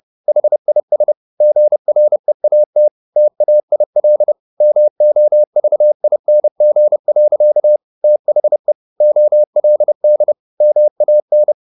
Morse code, heard as a keyed tone, about 30 wpm, 600 Hz. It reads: HIS GREAT TAIL MOVING. THE OLD MAN